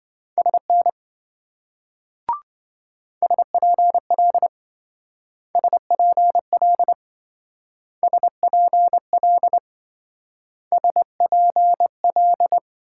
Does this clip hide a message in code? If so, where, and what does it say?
Yes: Morse code from 0.4 to 12.6 s, heard as a keyed tone, about 26 wpm, 700 Hz. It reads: SN SPL SPL SPL SPL